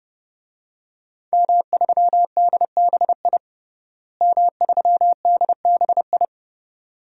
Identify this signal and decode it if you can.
Morse code, heard as a keyed tone, about 30 wpm, 700 Hz. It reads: M3DBI M3DBI
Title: DXCC-100 Call Signs - 5 Characters - Repeated 2x - 30wpm